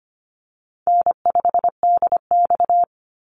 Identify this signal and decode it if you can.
Morse code, heard as a keyed tone, about 25 wpm, 700 Hz. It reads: N5DX
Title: Call Signs Repeated 3x - 25wpm